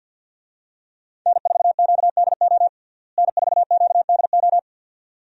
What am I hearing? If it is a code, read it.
Morse code, heard as a keyed tone, about 51 wpm, 700 Hz. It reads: N4QDO N4QDO
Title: USA Call Signs - 5 Characters - Repeated 2x - 50wpm